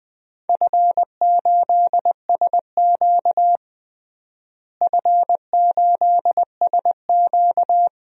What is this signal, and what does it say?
Morse code, heard as a keyed tone, about 21 wpm, 700 Hz. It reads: F8SQ F8SQ
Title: DXCC-100 Call Signs - 4 Characters - Repeated 2x - 20wpm